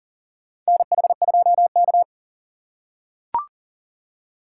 Morse code, heard as a keyed tone, about 40 wpm, 700 Hz. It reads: NH2K
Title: USA Call Signs - 2x Character Spacing - 40wpm